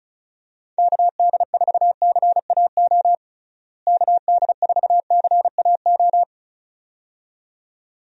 Morse code, heard as a keyed tone, about 36 wpm, 700 Hz. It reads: KD4CAO KD4CAO
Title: USA Call Signs - 6 Characters - Repeated 2x - 35wpm